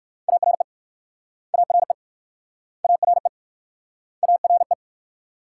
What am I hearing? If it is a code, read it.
Morse code, heard as a keyed tone, about 59 wpm, 700 Hz. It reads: ARE ARE ARE ARE